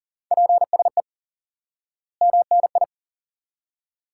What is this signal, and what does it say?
Morse code, heard as a keyed tone, about 40 wpm, 700 Hz. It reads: PSE MNI
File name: IQE Course 3 - Lesson 89 - YAGI - Two Elements - 40wpm